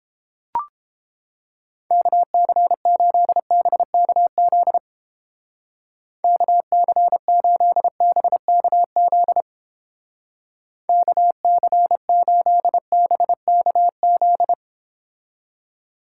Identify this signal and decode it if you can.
Morse code, heard as a keyed tone, about 31 wpm, 700 Hz. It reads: KC8BKZ KC8BKZ KC8BKZ